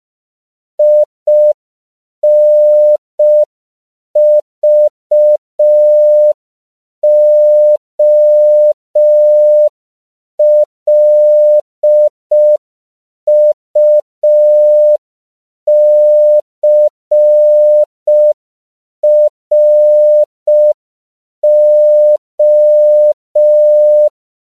Morse code, heard as a keyed tone, about 5 wpm, 600 Hz. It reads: INVOLUCRO
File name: CW / 5PPM